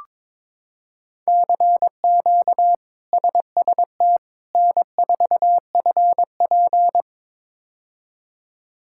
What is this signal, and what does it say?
Morse code, heard as a keyed tone, about 22 wpm, 700 Hz. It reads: CQ SST N4FP
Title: K1USN SST QSO Practice - 22wpm